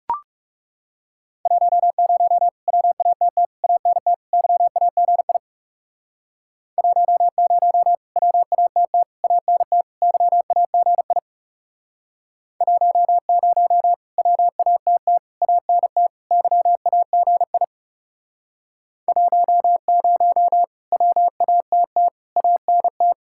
Morse code, heard as a keyed tone, about 37 wpm, 700 Hz. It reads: 10 WATT ANT YAGI 10 WATT ANT YAGI 10 WATT ANT YAGI 10 WATT ANT
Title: Speed-Racing Common QSO Phrases - 30wpm